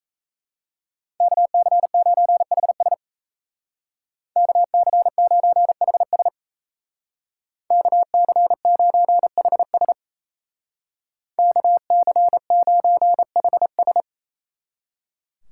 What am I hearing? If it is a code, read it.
Morse code, heard as a keyed tone, about 37 wpm, 700 Hz. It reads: KC9HS KC9HS KC9HS KC9HS